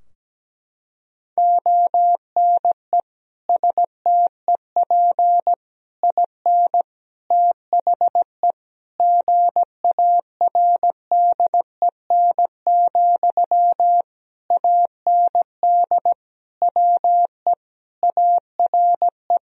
Morse code, heard as a keyed tone, about 17 wpm, 700 Hz. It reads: ONE STEP IN THE GARDEN, AND WE ARE
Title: Sentences from Top 1000 Words - Review Part 4 - 17wpm